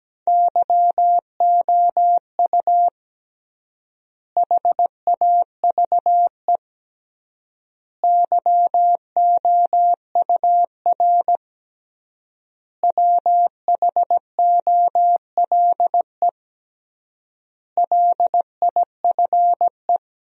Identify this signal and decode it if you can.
Morse code, heard as a keyed tone, about 17 wpm, 700 Hz. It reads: YOU HAVE YOUR WHOLE LIFE
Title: 3x Word-Spacing - Sentences from Top 400 Words - 17wpm